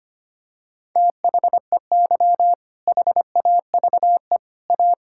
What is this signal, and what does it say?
Morse code, heard as a keyed tone, about 25 wpm, 700 Hz. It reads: THEY HAVE A
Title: Sentences from Top 1000 Words - Review Part 5 - 25wpm